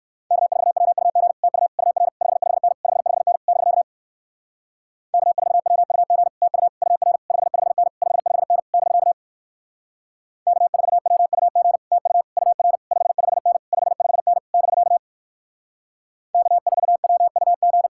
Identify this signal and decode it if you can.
Morse code, heard as a keyed tone, about 62 wpm, 700 Hz. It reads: K4WUG TU UR 55N 55N <BK> K4WUG TU UR 55N 55N <BK> K4WUG TU UR 55N 55N <BK> K4WUG